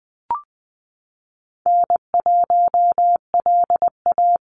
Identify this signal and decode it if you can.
Morse code, heard as a keyed tone, about 20 wpm, 700 Hz. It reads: N1LA
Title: USA Call Signs 20wpm